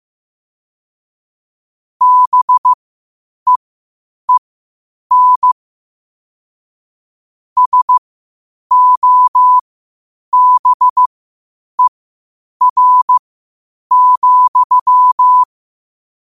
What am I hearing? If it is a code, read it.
Morse code, heard as a keyed tone, about 15 wpm, spaced as for 8 wpm, 1000 Hz. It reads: BEEN SOBER,
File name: qotdfiles / 8wpm